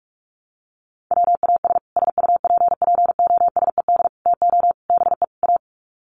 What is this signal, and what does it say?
Morse code, heard as a keyed tone, about 44 wpm, 700 Hz. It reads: WAS SUPPOSED TO BE A
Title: Sets of 5 Words Encore 45wpm